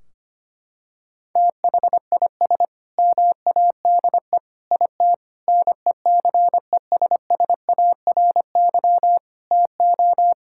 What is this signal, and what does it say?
Morse code, heard as a keyed tone, about 25 wpm, 700 Hz. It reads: THIS MADE IT NECESSARY TO
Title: Sentences from Top 1600 Words 25wpm